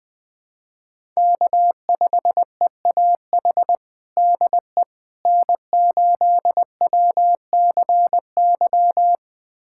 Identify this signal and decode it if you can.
Morse code, heard as a keyed tone, about 21 wpm, 700 Hz. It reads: K5EAH DE N8WCY
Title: USA Callsign DE Callsign - 5 Characters - 20wpm